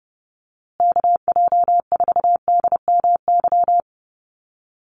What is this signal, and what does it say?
Morse code, heard as a keyed tone, about 30 wpm, 700 Hz. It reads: KJ4DMY
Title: USA Call Signs Repeated 2x 30wpm